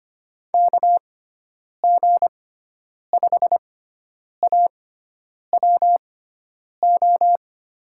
Morse code, heard as a keyed tone, about 25 wpm, spaced as for 9 wpm, 700 Hz. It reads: KG5AWO
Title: USA Call Signs - 6 Characters - 6x Character Spacing - 25wpm